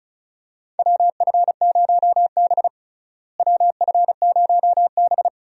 Morse code, heard as a keyed tone, about 34 wpm, 700 Hz. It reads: WF0B WF0B
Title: USA Call Signs - 4 Characters - Repeated 3x - 35wpm